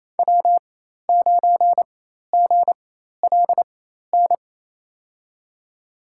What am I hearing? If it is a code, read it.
Morse code, heard as a keyed tone, about 28 wpm, spaced as for 13 wpm, 700 Hz. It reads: W9GLN
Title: USA Call Signs - 5 Characters - 4x Character Spacing - 28wpm